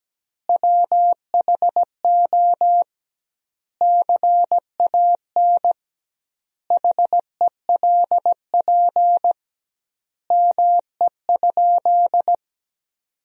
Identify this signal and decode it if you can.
Morse code, heard as a keyed tone, about 17 wpm, 700 Hz. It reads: WHO CAN HELP ME?
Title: 2x Word-Spacing - Sentences from Top 200 Words - 17wpm